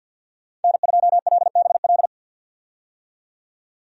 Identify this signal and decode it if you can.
Morse code, heard as a keyed tone, about 49 wpm, 700 Hz. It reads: N2LBL
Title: USA Call Signs - 5 Characters - 50wpm